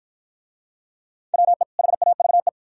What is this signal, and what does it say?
Morse code, heard as a keyed tone, about 50 wpm, 700 Hz. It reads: WE HAVE